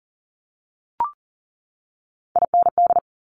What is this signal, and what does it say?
Morse code, heard as a keyed tone, about 38 wpm, 700 Hz. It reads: IND
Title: Multi-letter Phonemes 40wpm